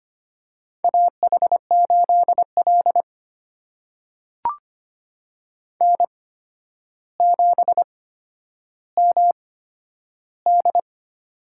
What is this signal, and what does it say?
Morse code, heard as a keyed tone, about 25 wpm, spaced as for 11 wpm, 700 Hz. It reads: AH8L N7MD
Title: USA Call Signs - 4 Characters - 8x Character Spacing - 25wpm